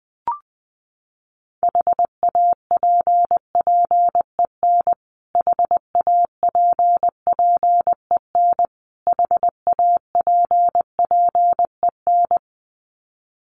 Morse code, heard as a keyed tone, about 20 wpm, 700 Hz. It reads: HAPPEN HAPPEN HAPPEN
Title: Top 500 Words Repeated 20wpm